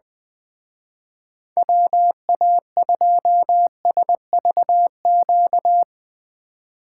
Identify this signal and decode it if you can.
Morse code, heard as a keyed tone, about 21 wpm, 700 Hz. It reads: WA2SVQ
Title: Speed Racing - USA 2x3 Call Signs 20wpm